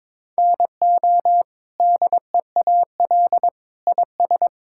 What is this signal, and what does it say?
Morse code, heard as a keyed tone, about 22 wpm, 700 Hz. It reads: NO DEAL IS
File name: practice / 22wpm / Sets of 8 Words - 22wpm